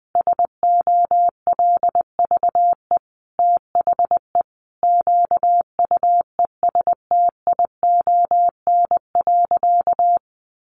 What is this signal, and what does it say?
Morse code, heard as a keyed tone, about 20 wpm, 700 Hz. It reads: SOLVE THE QUESTION.
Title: Sentences from Top 1000 Words 20wpm